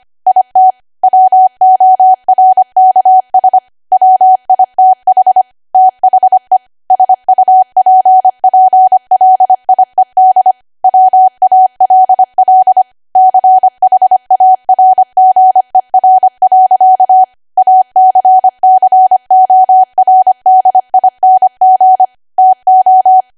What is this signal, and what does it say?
Morse code, heard as a keyed tone, about 25 wpm, 750 Hz. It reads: IT WORKS WITH THE SUPPLIED WALL CHARGER. ACCORDING TO
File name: Archive / 25 WPM / 25WPM